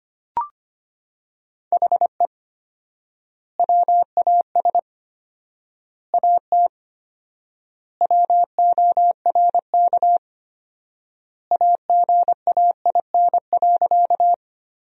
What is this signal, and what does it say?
Morse code, heard as a keyed tone, about 25 wpm, 700 Hz. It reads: HE WAS AT WORK AGAIN.
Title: 4x Word-Spacing - Sentences from Top 200 Words - 25wpm